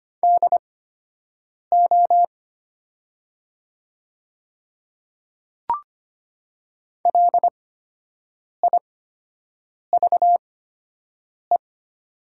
Morse code, heard as a keyed tone, about 25 wpm, spaced as for 7 wpm, 700 Hz. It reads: DO LIVE
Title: 8x Character Spacing - Top 200 Words - 25wpm